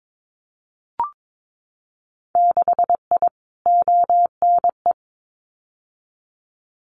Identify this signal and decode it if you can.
Morse code, heard as a keyed tone, about 22 wpm, 700 Hz. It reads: 6I ONE
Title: Winter Field Day Exchanges Encore 22wpm